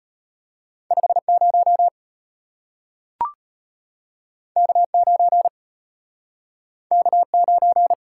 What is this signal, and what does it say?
Morse code, heard as a keyed tone, about 34 wpm, 700 Hz. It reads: 50 K9 K9